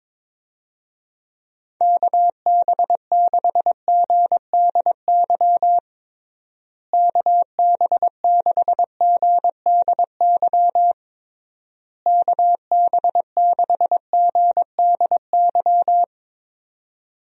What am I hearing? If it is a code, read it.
Morse code, heard as a keyed tone, about 22 wpm, 700 Hz. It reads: KB6GDY KB6GDY KB6GDY
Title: USA Call Signs - 6 Characters - Repeated 3x - 22wpm